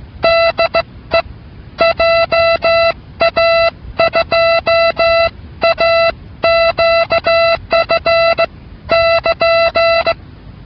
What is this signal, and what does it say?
Morse code, heard as a keyed tone, about 16 wpm, spaced as for 14 wpm, 670 Hz. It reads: DE JA2AQF <KN>